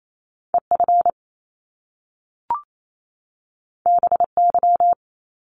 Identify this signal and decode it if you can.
Morse code, heard as a keyed tone, about 28 wpm, 700 Hz. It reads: EF BY
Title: Top Two Letter Combinations in Words 28wpm